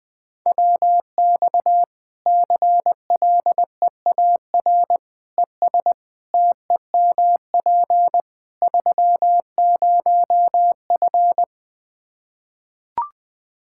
Morse code, heard as a keyed tone, about 20 wpm, 700 Hz. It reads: WX CLEAR ES TEMP 30F
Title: Practice QSO - Exchange 2 - Info - 20wpm